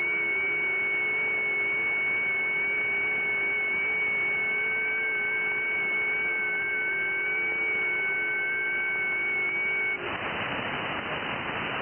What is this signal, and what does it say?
FT8.
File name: IC-7100